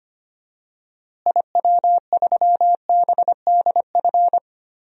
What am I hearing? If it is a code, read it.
Morse code, heard as a keyed tone, about 25 wpm, 700 Hz. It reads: IW3BDF
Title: DXCC-100 Call Signs - 6 Characters - Repeated 3x - 25wpm